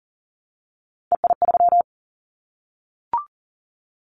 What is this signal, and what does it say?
Morse code, heard as a keyed tone, about 41 wpm, 700 Hz. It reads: EI3
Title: Callsign Prefixes for Top 50 DXCC - 40wpm